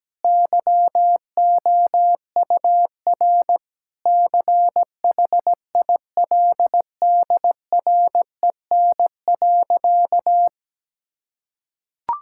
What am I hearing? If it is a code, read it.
Morse code, heard as a keyed tone, about 17 wpm, 700 Hz. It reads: YOUR CHILDREN.
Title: Sentences from Top 1000 Words - Review Part 6 - 17wpm